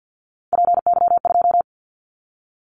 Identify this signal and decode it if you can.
Morse code, heard as a keyed tone, about 50 wpm, 700 Hz. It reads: PY2
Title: Callsign Prefixes for Top 50 DXCC - 50wpm